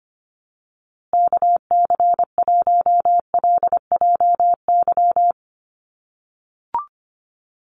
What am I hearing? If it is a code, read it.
Morse code, heard as a keyed tone, about 25 wpm, 700 Hz. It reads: KC1LJY
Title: USA Call Signs Repeated 3x 25wpm